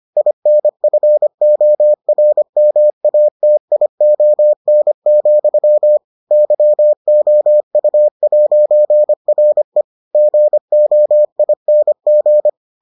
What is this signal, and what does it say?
Morse code, heard as a keyed tone, about 25 wpm, 600 Hz. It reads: INFORMATION, YOU'RE GOING